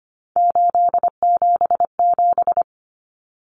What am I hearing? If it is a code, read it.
Morse code, heard as a keyed tone, about 25 wpm, 700 Hz. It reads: 877
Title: Common Cut Numbers - 3-Digit 25wpm